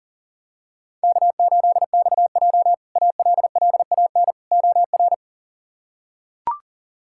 Morse code, heard as a keyed tone, about 40 wpm, 700 Hz. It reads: K8XJ ALLAN OR
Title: K1USN SST QSO Practice - 40wpm